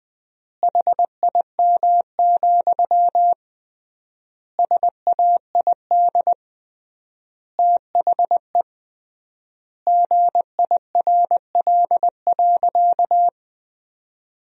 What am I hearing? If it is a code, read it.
Morse code, heard as a keyed tone, about 20 wpm, 700 Hz. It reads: HIM, SAID THE GIRL.
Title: 3x Word-Spacing - Sentences from Top 400 Words - 20wpm